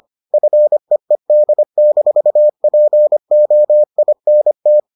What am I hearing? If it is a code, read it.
Morse code, heard as a keyed tone, about 25 wpm, 600 Hz. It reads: FEED-POINT